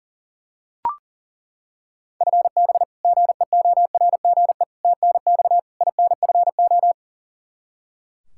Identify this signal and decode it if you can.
Morse code, heard as a keyed tone, about 40 wpm, 700 Hz. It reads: FB GEORGE TNX INFO
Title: K1USN SST QSO Practice 40wpm